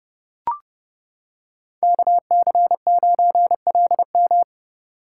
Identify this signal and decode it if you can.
Morse code, heard as a keyed tone, about 28 wpm, 700 Hz. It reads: KC9LM